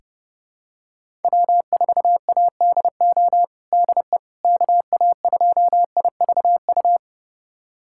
Morse code, heard as a keyed tone, about 30 wpm, 700 Hz. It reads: W4ADO DE KA2IVU
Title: USA Callsign DE Callsign 30wpm